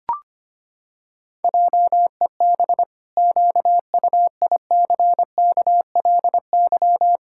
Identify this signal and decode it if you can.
Morse code, heard as a keyed tone, about 25 wpm, 700 Hz. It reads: JEB QUICKLY
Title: Pangram Challenge v2 - 25wpm